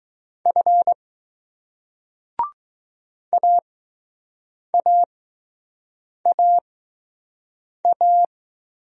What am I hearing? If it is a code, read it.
Morse code, heard as a keyed tone, about 20 wpm, 700 Hz. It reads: F A A A A